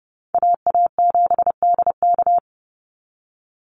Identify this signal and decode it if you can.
Morse code, heard as a keyed tone, about 30 wpm, 700 Hz. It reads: AA7DK
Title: NA SOTA Chaser Callsigns 30wpm